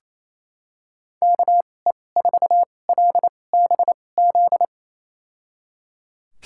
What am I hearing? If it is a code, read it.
Morse code, heard as a keyed tone, about 28 wpm, spaced as for 20 wpm, 690 Hz. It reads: KE4LBZ
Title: USA Call Signs - 6 Characters - 2x Character Spacing - 28wpm